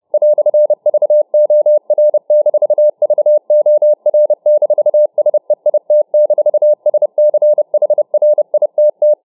Morse code, heard as a keyed tone, about 30 wpm, 600 Hz. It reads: "VOR-VOR-SEIT-SCHRITT